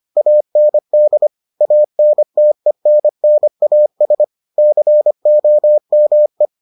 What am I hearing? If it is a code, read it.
Morse code, heard as a keyed tone, about 25 wpm, 600 Hz. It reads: AND ANTENNAS COME